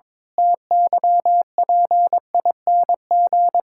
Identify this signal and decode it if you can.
Morse code, heard as a keyed tone, about 22 wpm, 700 Hz. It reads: TYPING
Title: Pangram Challenge v2 - 22wpm